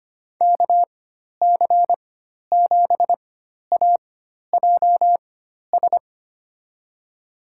Morse code, heard as a keyed tone, about 25 wpm, spaced as for 12 wpm, 700 Hz. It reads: KC7AJS
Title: USA Call Signs - 6 Characters - 4x Character Spacing - 25wpm